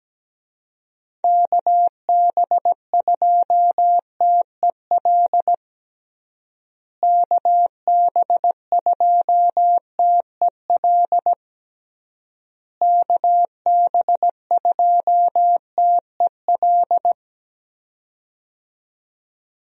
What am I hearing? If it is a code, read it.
Morse code, heard as a keyed tone, about 17 wpm, 700 Hz. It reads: KB2TEL KB2TEL KB2TEL
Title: USA Call Signs - 6 Characters - Repeated 3x - 17wpm